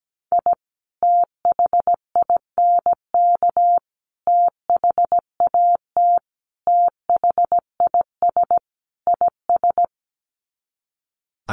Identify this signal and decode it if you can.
Morse code, heard as a keyed tone, about 17 wpm, 700 Hz. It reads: I THINK THAT THIS IS
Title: Sets of 5 Words Encore2 17wpm